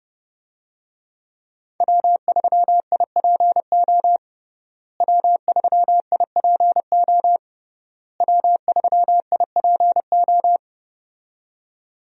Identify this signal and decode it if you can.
Morse code, heard as a keyed tone, about 30 wpm, 700 Hz. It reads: W3IPO W3IPO W3IPO
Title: USA Call Signs - 5 Characters - Repeated 3x - 30wpm